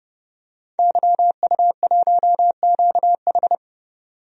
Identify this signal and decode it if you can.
Morse code, heard as a keyed tone, about 30 wpm, 700 Hz. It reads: YU1QH
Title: DXCC-100 Call Signs - 5 Characters - 30wpm